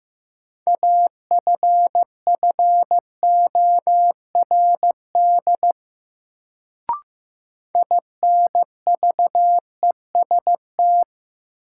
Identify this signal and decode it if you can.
Morse code, heard as a keyed tone, about 16 wpm, 700 Hz. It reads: AFFORD INVEST
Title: Sentences from Top 1100 Words - New Word Drill - 15wpm